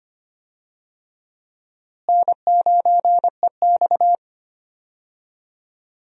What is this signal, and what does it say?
Morse code, heard as a keyed tone, about 25 wpm, 700 Hz. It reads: N9EX
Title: USA Call Signs - 4 Characters - Repeated 3x - 25wpm